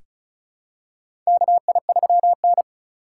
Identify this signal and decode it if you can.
Morse code, heard as a keyed tone, about 36 wpm, 700 Hz. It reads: KI3N
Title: USA Call Signs - 4x Character Spacing - 35wpm